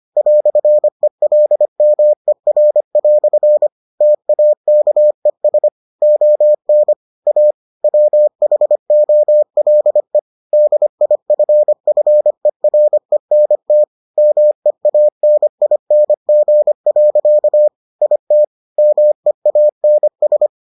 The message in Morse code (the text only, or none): "ELMER" TAKES ON A WHOLE DIFFERENT MEANING. IT MEANS